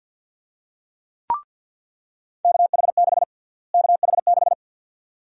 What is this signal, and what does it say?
Morse code, heard as a keyed tone, about 50 wpm, 700 Hz. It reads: KH6 KH6
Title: USA Call Sign Prefixes - Repeated 2x - 50wpm